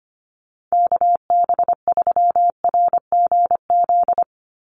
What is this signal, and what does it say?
Morse code, heard as a keyed tone, about 25 wpm, 700 Hz. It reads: KB3RGZ
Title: USA Call Signs 25wpm